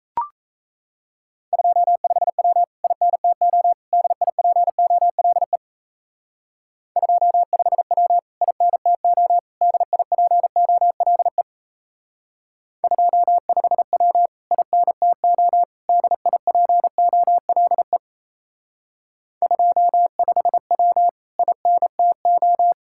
Morse code, heard as a keyed tone, about 35 wpm, 700 Hz. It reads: 25W INTO DIPOLE 25W INTO DIPOLE 25W INTO DIPOLE 25W INTO
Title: Speed-Racing Common QSO Phrases - 28wpm